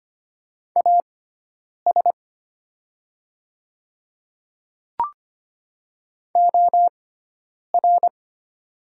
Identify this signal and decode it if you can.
Morse code, heard as a keyed tone, about 25 wpm, spaced as for 9 wpm, 700 Hz. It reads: AS OR
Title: 6x Character Spacing - Top 100 Words - 25wpm